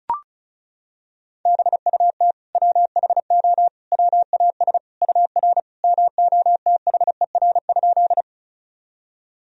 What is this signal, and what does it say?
Morse code, heard as a keyed tone, about 36 wpm, 700 Hz. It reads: BUT WHO WAS UR MOTHER?
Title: Sentences from Top 200 Words and Abbreviations 35wpm